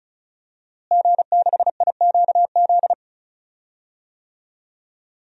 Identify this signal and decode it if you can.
Morse code, heard as a keyed tone, about 36 wpm, 700 Hz. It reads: G6IQZ
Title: DXCC-100 Call Signs - 5 Characters - Repeated 2x - 35wpm